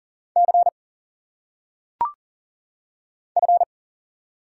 Morse code, heard as a keyed tone, about 40 wpm, 700 Hz. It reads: C F
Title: Single Letters - Phonetic Pronunciation 40wpm